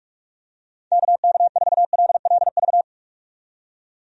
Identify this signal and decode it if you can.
Morse code, heard as a keyed tone, about 46 wpm, 700 Hz. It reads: KK4LLV
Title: USA Call Signs - 6 Characters - 45wpm